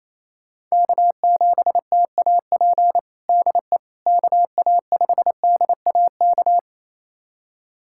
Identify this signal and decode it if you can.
Morse code, heard as a keyed tone, about 27 wpm, 700 Hz. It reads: K7TAP DE KA5DAK
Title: USA Callsign DE Callsign 28wpm